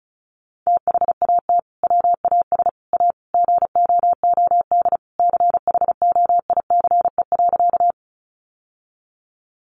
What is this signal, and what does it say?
Morse code, heard as a keyed tone, about 34 wpm, 700 Hz. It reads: THAT WAS A GOOD CHOICE.
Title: Sentences from Top 900 Words 35wpm